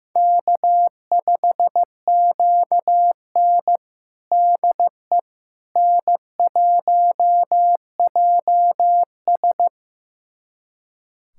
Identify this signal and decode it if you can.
Morse code, heard as a keyed tone, about 15 wpm, 700 Hz. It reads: K5QN DE N1JS
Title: USA Callsign DE Callsign - 4 Characters - 15wpm